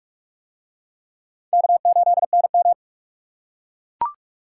Morse code, heard as a keyed tone, about 41 wpm, 700 Hz. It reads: K8NM